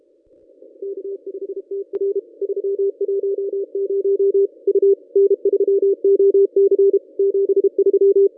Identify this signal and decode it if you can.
Morse code, heard as a keyed tone, about 32 wpm, 400 Hz. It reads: K5TR 310 U N3OC 73
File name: contests / ss-cw-k5tr-op-n5rz -1100